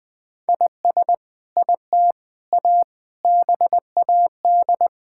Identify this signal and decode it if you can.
Morse code, heard as a keyed tone, about 20 wpm, 700 Hz. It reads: IS IT A BAD
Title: Sentences from Top 1000 Words - Review Part 5 - 20wpm